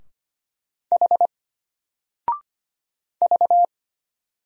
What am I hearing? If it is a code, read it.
Morse code, heard as a keyed tone, about 25 wpm, 700 Hz. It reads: H V
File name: Single-Letter-Number-25wpm